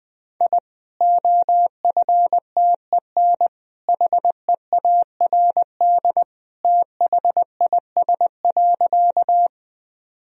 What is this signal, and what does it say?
Morse code, heard as a keyed tone, about 20 wpm, 700 Hz. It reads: I OFTEN HEARD THIS.
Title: Sentences from Top 1000 Words - Review Part 1 - 20wpm